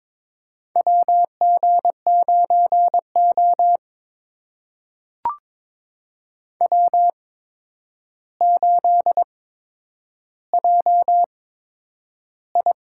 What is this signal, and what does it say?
Morse code, heard as a keyed tone, about 22 wpm, spaced as for 9 wpm, 700 Hz. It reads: WG9O W8JI
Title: USA Call Signs - 4 Characters - 8x Character Spacing - 22wpm